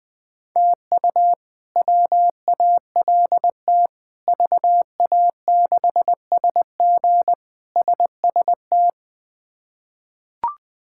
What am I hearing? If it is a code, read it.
Morse code, heard as a keyed tone, about 21 wpm, 700 Hz. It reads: TU WALT VA6SG SST
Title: K1USN SST QSO Practice - 20wpm